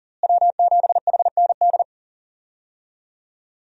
Morse code, heard as a keyed tone, about 40 wpm, 700 Hz. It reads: W7HND
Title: USA Call Signs - 5 Characters - Repeated 2x - 40wpm